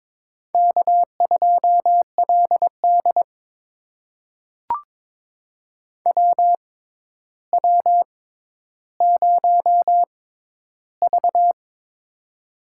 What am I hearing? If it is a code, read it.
Morse code, heard as a keyed tone, about 22 wpm, spaced as for 11 wpm, 700 Hz. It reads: K2LD WW0V
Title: USA Call Signs - 4 Characters - 6x Character Spacing - 22wpm